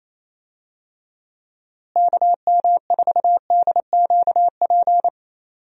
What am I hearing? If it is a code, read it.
Morse code, heard as a keyed tone, about 28 wpm, 700 Hz. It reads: KM4DQP
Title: USA Call Signs - 6 Characters - Repeated 3x - 28wpm